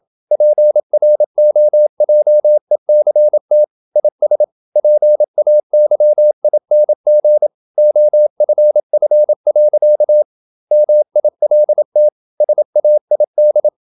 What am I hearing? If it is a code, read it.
Morse code, heard as a keyed tone, about 28 wpm, 600 Hz. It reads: PROJECT IS PAYING OFF. MILT SAID
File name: Never 0027wpm